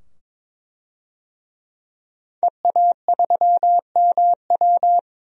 Morse code, heard as a keyed tone, about 22 wpm, 700 Hz. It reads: EA3MW
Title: DXCC-100 Call Signs - 5 Characters - 22wpm